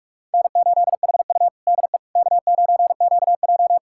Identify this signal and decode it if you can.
Morse code, heard as a keyed tone, about 46 wpm, 700 Hz. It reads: N8HU DE K9QJ
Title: USA Callsign DE Callsign - 4 Characters - 45wpm